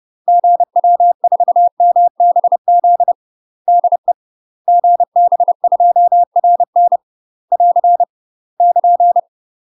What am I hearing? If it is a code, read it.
Morse code, heard as a keyed tone, about 30 wpm, 700 Hz. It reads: GW4MBZ DE GB2RN <AR> <KN>
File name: UK MASTER QSOs TRAINING / 30WPM / UK MASTER QSOs 0018